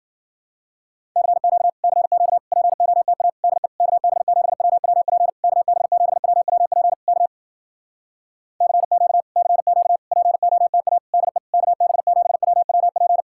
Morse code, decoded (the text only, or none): CQ CQ POTA DE KB7WWP KB7WWP K CQ CQ POTA DE KB7WWP